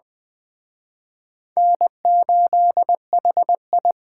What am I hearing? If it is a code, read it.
Morse code, heard as a keyed tone, about 20 wpm, 700 Hz. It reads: N8HI
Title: Speed Racing - USA 1x2 Call Signs 17wpm